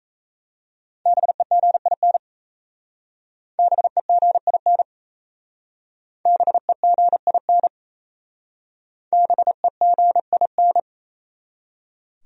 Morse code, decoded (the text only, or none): BEGIN BEGIN BEGIN BEGIN